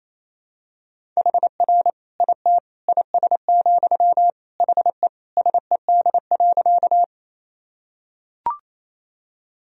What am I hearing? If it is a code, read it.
Morse code, heard as a keyed tone, about 28 wpm, 700 Hz. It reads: HR IT IS, HE SED.
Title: Sentences from Top 200 Words and Abbreviations 28wpm